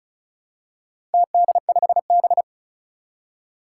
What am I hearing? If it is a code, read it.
Morse code, heard as a keyed tone, about 35 wpm, 700 Hz. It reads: TD5B
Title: Four Challenging Letter-Number Combinations 35wpm